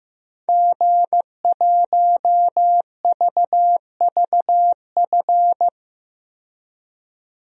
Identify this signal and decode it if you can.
Morse code, heard as a keyed tone, about 15 wpm, 700 Hz. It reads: G1VVF
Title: DXCC-100 Call Signs - 5 Characters - 15wpm